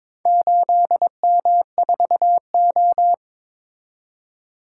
Morse code, heard as a keyed tone, about 22 wpm, 700 Hz. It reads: 8M4O
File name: practice / 22wpm / Four Letter-Number Combinations - 22wpm